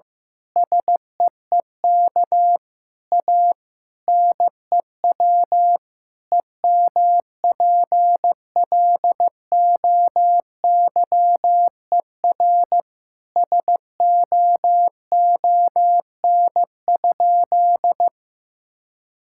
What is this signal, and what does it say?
Morse code, heard as a keyed tone, about 15 wpm, 700 Hz. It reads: SEEK A NEW EMPLOYER SOON?
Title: Sentences from Top 1600 Words 15wpm